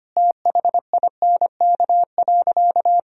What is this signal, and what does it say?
Morse code, heard as a keyed tone, about 25 wpm, 700 Hz. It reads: THINK.
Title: Sentences from Top 1000 Words - Review Part 2 - 25wpm